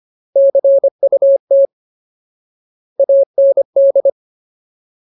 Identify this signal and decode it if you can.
Morse code, heard as a keyed tone, about 25 wpm, 550 Hz. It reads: CUT AND
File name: Ultimate Word List / 25WPM Ultimate